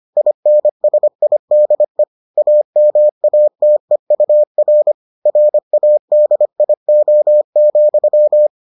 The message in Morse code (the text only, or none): INSIDE AMATEUR RADIO,